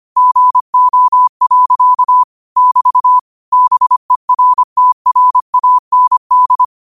Morse code, heard as a keyed tone, about 25 wpm, 1000 Hz. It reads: GO. <BT> BERTRAND
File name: qotdfiles / 25wpm